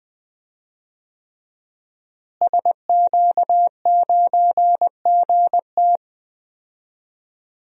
Morse code, heard as a keyed tone, about 20 wpm, 700 Hz. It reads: SQ9GT
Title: DXCC-100 Call Signs - 5 Characters - Repeated 3x - 20wpm